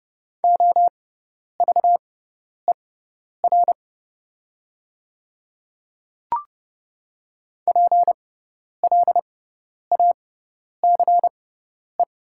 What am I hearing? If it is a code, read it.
Morse code, heard as a keyed tone, about 30 wpm, spaced as for 10 wpm, 700 Hz. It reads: OVER PLACE
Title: 6x Character Spacing - Top 200 Words - 30wpm